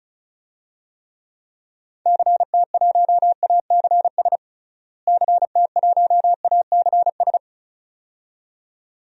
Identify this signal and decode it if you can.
Morse code, heard as a keyed tone, about 36 wpm, 700 Hz. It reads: CT1ACS CT1ACS
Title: DXCC-100 Call Signs - 6 Characters - Repeated 2x - 35wpm